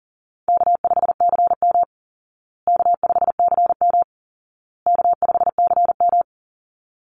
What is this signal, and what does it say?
Morse code, heard as a keyed tone, about 40 wpm, 700 Hz. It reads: K5CM K5CM K5CM
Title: Call Signs Repeated 2x - 40wpm